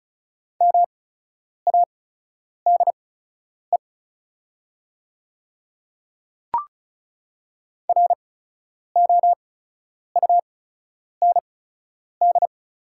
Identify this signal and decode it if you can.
Morse code, heard as a keyed tone, about 36 wpm, spaced as for 10 wpm, 700 Hz. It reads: MADE ROUND
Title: 8x Character Spacing - Top 200 Words - 35wpm